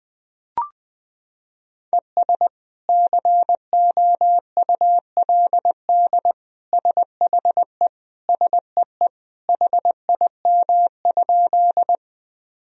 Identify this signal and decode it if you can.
Morse code, heard as a keyed tone, about 20 wpm, 700 Hz. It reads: ES COULD SHE SEE HIM?
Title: Sentences from Top 100 Words and Abbreviations 20wpm